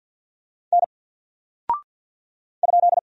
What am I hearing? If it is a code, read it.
Morse code, heard as a keyed tone, about 50 wpm, 700 Hz. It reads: N ?
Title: Single Character (Letter:Number:Punctuation:Prosign) 50wpm